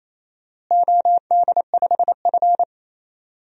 Morse code, heard as a keyed tone, about 28 wpm, 700 Hz. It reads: OD5F
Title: DXCC-100 Call Signs - 4 Characters - Repeated 3x - 28wpm